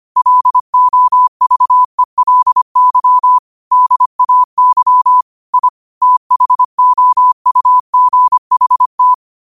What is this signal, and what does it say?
Morse code, heard as a keyed tone, about 25 wpm, 1000 Hz. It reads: LOVELY DAY I THOUGHT
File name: qotdfiles / 25wpm